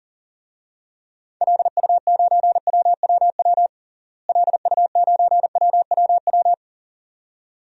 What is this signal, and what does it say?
Morse code, heard as a keyed tone, about 40 wpm, 700 Hz. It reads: LU9WWW LU9WWW
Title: DXCC-100 Call Signs - 6 Characters - Repeated 2x - 40wpm